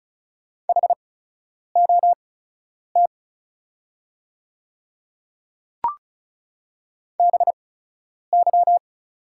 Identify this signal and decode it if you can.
Morse code, heard as a keyed tone, about 36 wpm, spaced as for 10 wpm, 690 Hz. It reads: HOT BY
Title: 8x Character Spacing - Top 100 Words - 35wpm